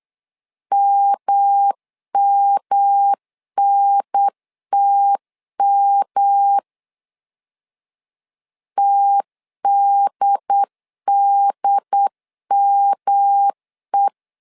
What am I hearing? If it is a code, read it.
Morse code, heard as a keyed tone, about 8 wpm, 800 Hz. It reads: MMNTM TDDME
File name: LEZIONI / 3 MWRO / 3 M 8wpm 2300